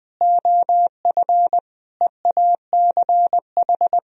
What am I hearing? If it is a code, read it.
Morse code, heard as a keyed tone, about 20 wpm, 700 Hz. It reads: OF EACH
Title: Sentences from Top 1000 Words - Review Part 1 - 20wpm